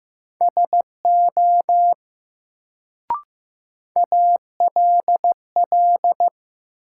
Morse code, heard as a keyed tone, about 15 wpm, 700 Hz. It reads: SO ALL
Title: Sets of 2 Words - New Word Drill - 15wpm